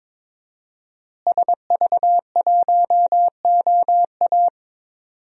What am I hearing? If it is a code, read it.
Morse code, heard as a keyed tone, about 22 wpm, 700 Hz. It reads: SV1OA
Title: DXCC-100 Call Signs - 5 Characters - Repeated 3x - 22wpm